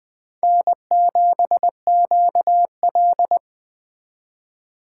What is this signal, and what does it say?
Morse code, heard as a keyed tone, about 21 wpm, 700 Hz. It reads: N7QL